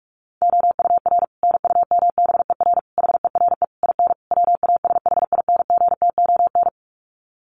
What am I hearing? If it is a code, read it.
Morse code, heard as a keyed tone, about 44 wpm, 700 Hz. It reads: OUR NUMBER HERE IN WASHINGTON
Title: Sets of 5 Words Encore2 45wpm